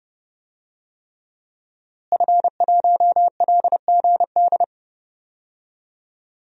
Morse code, heard as a keyed tone, about 30 wpm, 700 Hz. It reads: F1LGD
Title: DXCC-100 Call Signs - 5 Characters - Repeated 3x - 30wpm